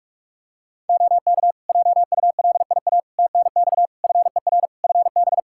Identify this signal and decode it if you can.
Morse code, heard as a keyed tone, about 46 wpm, 700 Hz. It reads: OK JULIA TNX FER FB
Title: Practice QSO - Exchange 4 - Ending - 45wpm